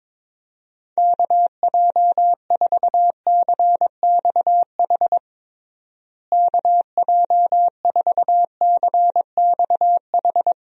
Morse code, heard as a keyed tone, about 22 wpm, 700 Hz. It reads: KJ4CXH KJ4CXH
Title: USA Call Signs - 6 Characters - Repeated 3x - 22wpm